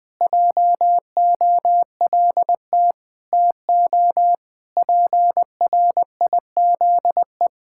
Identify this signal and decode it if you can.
Morse code, heard as a keyed tone, about 19 wpm, 700 Hz. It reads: JOLT TO PRIZE
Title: Pangram Challenge v2 - 20wpm